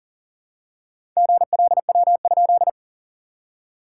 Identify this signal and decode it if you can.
Morse code, heard as a keyed tone, about 40 wpm, 660 Hz. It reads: GLW?
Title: SOTA QSO Practice 40wpm